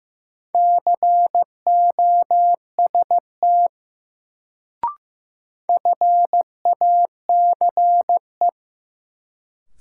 Morse code, heard as a keyed tone, about 15 wpm, 700 Hz. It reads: COST FACE
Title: Sets of 3 Words - New Word Drill - 15wpm